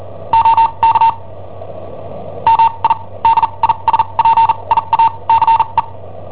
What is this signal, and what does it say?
Morse code, heard as a keyed tone, about 39 wpm, spaced as for 29 wpm, 910 Hz. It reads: OK MI DISPIACE
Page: Cassetta esercizi preparata da I7OHP per IW5ELP